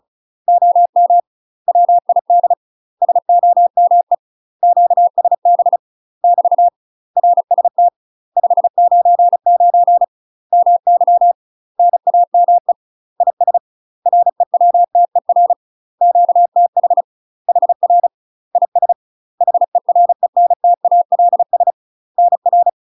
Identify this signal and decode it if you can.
Morse code, heard as a keyed tone, about 36 wpm, 700 Hz. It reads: OM WID SOME QSB <BT> RST 599 MY NAME IS REWTER QTH HR IS HERENTALS NR
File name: UK MASTER QSOs TRAINING / 35WPM / UK MASTER QSOs 0012